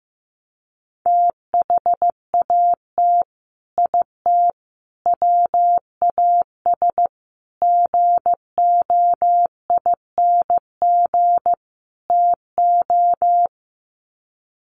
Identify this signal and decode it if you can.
Morse code, heard as a keyed tone, about 15 wpm, 700 Hz. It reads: THAT IT WAS GOING TO
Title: Sets of 5 Words Encore2 15wpm